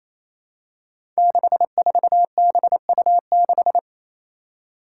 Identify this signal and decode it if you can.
Morse code, heard as a keyed tone, about 27 wpm, 700 Hz. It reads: 64BU6
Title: Five Challenging Letter-Number Combinations 28wpm